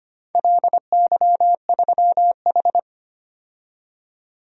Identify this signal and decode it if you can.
Morse code, heard as a keyed tone, about 25 wpm, 700 Hz. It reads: LY3H
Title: DXCC-100 Call Signs - 4 Characters - 25wpm